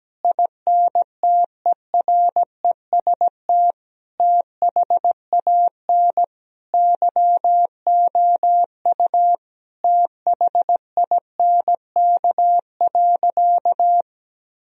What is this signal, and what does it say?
Morse code, heard as a keyed tone, about 17 wpm, 700 Hz. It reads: INTEREST THAN YOU THINK.
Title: Sentences from Top 1000 Words - Review Part 2 - 17wpm